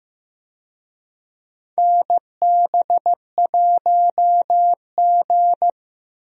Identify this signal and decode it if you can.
Morse code, heard as a keyed tone, about 15 wpm, 700 Hz. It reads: NB1G
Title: USA Call Signs - 4 Characters - Repeated 3x - 15wpm